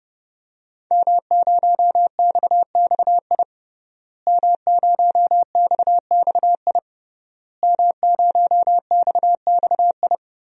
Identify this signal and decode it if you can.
Morse code, heard as a keyed tone, about 32 wpm, 700 Hz. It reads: M0XXI M0XXI M0XXI
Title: DXCC-100 Call Signs - 5 Characters - Repeated 3x - 30wpm